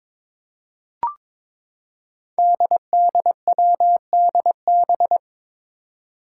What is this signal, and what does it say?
Morse code, heard as a keyed tone, about 22 wpm, 700 Hz. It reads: DDWDB
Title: Five Challenging Letter-Number Combinations 22wpm